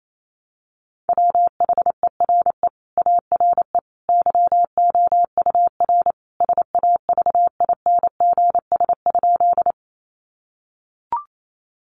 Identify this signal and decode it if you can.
Morse code, heard as a keyed tone, about 28 wpm, 700 Hz. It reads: WHERE ARE YOUR SAVINGS?
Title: Sentences from Top 1200 Words 28wpm